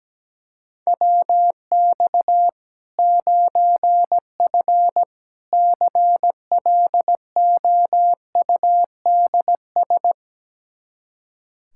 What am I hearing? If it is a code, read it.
Morse code, heard as a keyed tone, about 17 wpm, 700 Hz. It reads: WX 9F CLOUDS
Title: Practice QSO - Exchange 2 - Info - 17wpm